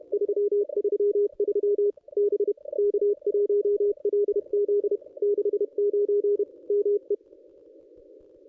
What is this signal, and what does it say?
Morse code, heard as a keyed tone, about 31 wpm, 410 Hz. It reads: 333 B K1LZ 69 ME
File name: contests / ss-cw-k5tr-op-n5rz -1200